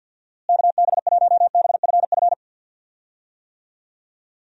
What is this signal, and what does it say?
Morse code, heard as a keyed tone, about 50 wpm, 700 Hz. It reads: KB1BFF
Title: USA Call Signs - 6 Characters - Repeated 3x - 50wpm